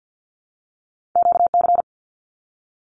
Morse code, heard as a keyed tone, about 50 wpm, 700 Hz. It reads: QC
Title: Canadian Provinces and Territories - Repeated 3x - 50wpm